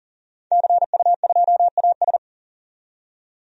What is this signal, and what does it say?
Morse code, heard as a keyed tone, about 40 wpm, 700 Hz. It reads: CU2AS
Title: DXCC-100 Call Signs - 5 Characters - 40wpm